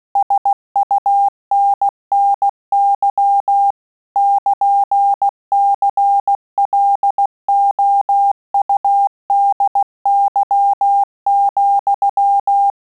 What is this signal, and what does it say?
Morse code, heard as a keyed tone, about 16 wpm, 800 Hz. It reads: SUNNY <KN>CLOUDY,
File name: QSO CW 75 CPM